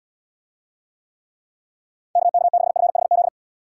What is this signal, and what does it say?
Morse code, heard as a keyed tone, about 79 wpm, 700 Hz. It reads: KK6FHZ